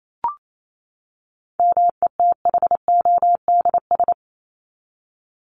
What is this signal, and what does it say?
Morse code, heard as a keyed tone, about 28 wpm, 700 Hz. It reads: METHODS
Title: ES:S Suffix - Top 100 Words with Base Words 28wpm